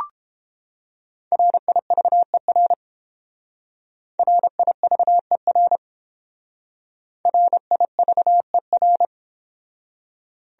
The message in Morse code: RIVER RIVER RIVER

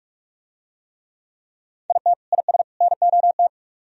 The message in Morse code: IT IS NOT

